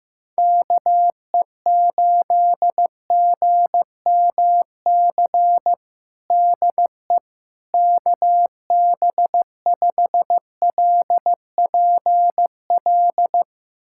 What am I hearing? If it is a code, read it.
Morse code, heard as a keyed tone, about 16 wpm, 700 Hz. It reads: KE8GMC DE KB5LPL
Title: USA Callsign DE Callsign - 6 Characters - 15wpm